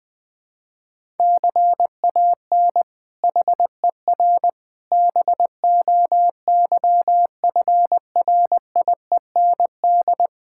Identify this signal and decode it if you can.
Morse code, heard as a keyed tone, about 20 wpm, 700 Hz. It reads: CAN HER BOYFRIEND
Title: Sentences from Top 1000 Words - Review Part 5 - 20wpm